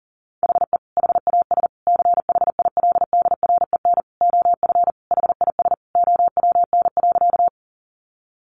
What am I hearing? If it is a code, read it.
Morse code, heard as a keyed tone, about 40 wpm, 700 Hz. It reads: HE HAS CHILDREN OF HIS OWN.
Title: Sentences from Top 300 Words 40wpm